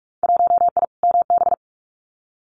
Morse code, heard as a keyed tone, about 46 wpm, 700 Hz. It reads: 1I MB
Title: Winter Field Day Exchanges Encore 45wpm